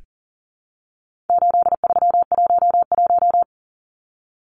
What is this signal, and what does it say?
Morse code, heard as a keyed tone, about 40 wpm, 700 Hz. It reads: 8311
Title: Numbers - 4-Digit 40wpm